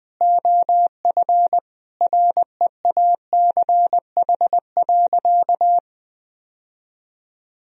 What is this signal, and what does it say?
Morse code, heard as a keyed tone, about 20 wpm, 700 Hz. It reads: OF REACH.
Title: Sentences from Top 1000 Words - Review Part 2 - 20wpm